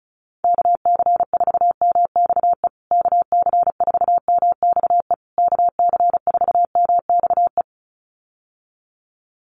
Morse code, heard as a keyed tone, about 34 wpm, 700 Hz. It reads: KC4MXE KC4MXE KC4MXE
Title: USA Call Signs Repeated 3x 35wpm